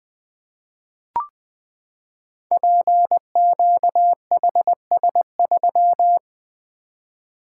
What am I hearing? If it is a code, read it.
Morse code, heard as a keyed tone, about 20 wpm, 700 Hz. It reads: PQHS3
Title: Code Groups - 5 Letter:Numbers - 20wpm